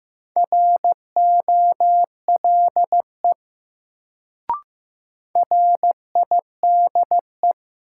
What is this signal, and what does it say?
Morse code, heard as a keyed tone, about 15 wpm, 700 Hz. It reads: ROLE RIDE
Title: Sentences from Top 1000 Words - Review Part 6 - New Word Drill - 15wpm